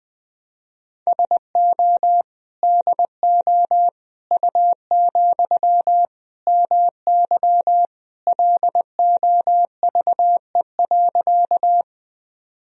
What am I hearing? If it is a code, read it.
Morse code, heard as a keyed tone, about 21 wpm, 700 Hz. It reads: SO DO U, MY LOVE.
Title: Sentences from Top 400 Words and Abbreviations 20wpm